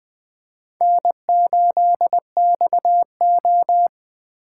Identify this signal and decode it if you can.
Morse code, heard as a keyed tone, about 20 wpm, 700 Hz. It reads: N8XO
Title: Speed Racing - USA 1x2 Call Signs 17wpm